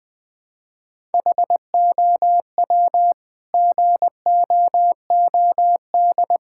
Morse code, heard as a keyed tone, about 20 wpm, 700 Hz. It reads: HOW GOOD
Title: Sentences from Top 1000 Words - Review Part 6 - 20wpm